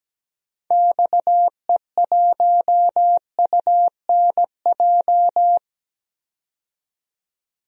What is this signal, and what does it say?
Morse code, heard as a keyed tone, about 17 wpm, 700 Hz. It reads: XE1UNJ
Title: DXCC-100 Call Signs - 6 Characters - 17wpm